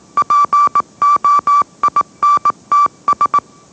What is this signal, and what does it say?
Morse code, heard as a keyed tone, about 18 wpm, 1160 Hz. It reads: POINTS